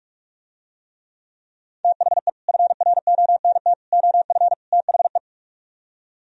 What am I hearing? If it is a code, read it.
Morse code, heard as a keyed tone, about 48 wpm, 700 Hz. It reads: THE FRONT OF THE